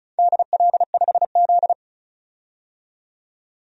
Morse code, heard as a keyed tone, about 36 wpm, 700 Hz. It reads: DL5Z
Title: DXCC-100 Call Signs - 4 Characters - 35wpm